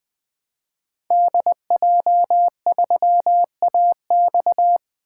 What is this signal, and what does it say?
Morse code, heard as a keyed tone, about 20 wpm, 700 Hz. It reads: DJ3AX
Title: 2023 SOTA Activator Call Signs - 20wpm